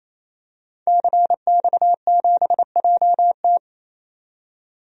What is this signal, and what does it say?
Morse code, heard as a keyed tone, about 28 wpm, 700 Hz. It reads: CX7JT
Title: DXCC-100 Call Signs - 5 Characters - 28wpm